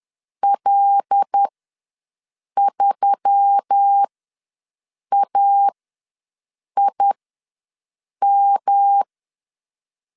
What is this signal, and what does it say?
Morse code, heard as a keyed tone, about 10 wpm, spaced as for 6 wpm, 800 Hz. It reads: L3AIM
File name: LEZIONI / 6 PKX0 / 6 PKX0 ws 10wpm 4220